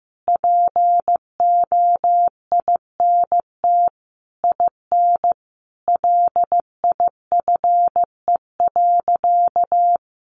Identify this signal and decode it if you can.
Morse code, heard as a keyed tone, about 15 wpm, 700 Hz. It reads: POINT IN LIFE.
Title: Sentences from Top 1200 Words 15wpm